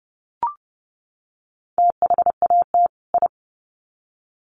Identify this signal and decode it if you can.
Morse code, heard as a keyed tone, about 30 wpm, 700 Hz. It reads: THAT I
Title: Sets of 2 Words 30wpm